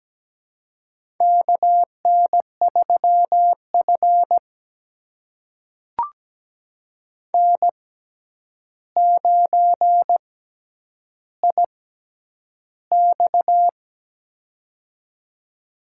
Morse code, heard as a keyed tone, about 17 wpm, spaced as for 9 wpm, 700 Hz. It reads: KN3F N9IX
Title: USA Call Signs - 4 Characters - 6x Character Spacing - 17wpm